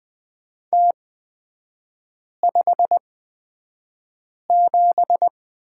Morse code, heard as a keyed tone, about 20 wpm, spaced as for 5 wpm, 690 Hz. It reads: T57
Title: 229 Lesson 37e - Letter x - Three Characters 20wpm